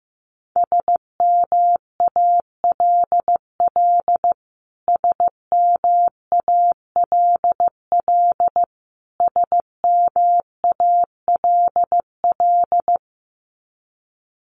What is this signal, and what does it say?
Morse code, heard as a keyed tone, about 15 wpm, 700 Hz. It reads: SMALL SMALL SMALL
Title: Top 500 Words Repeated 15wpm